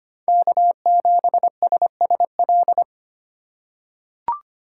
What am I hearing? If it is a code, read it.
Morse code, heard as a keyed tone, about 25 wpm, 700 Hz. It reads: K7SSL
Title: USA Call Signs - 4x Character Spacing - 25wpm